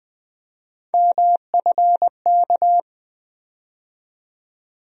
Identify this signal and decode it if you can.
Morse code, heard as a keyed tone, about 20 wpm, 700 Hz. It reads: MFK
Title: USA Call Sign Postfixes - 3 Letters 20wpm